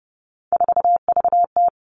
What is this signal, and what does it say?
Morse code, heard as a keyed tone, about 30 wpm, 700 Hz. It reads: 4VT
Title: All Cut Numbers - 3-Digit 30wpm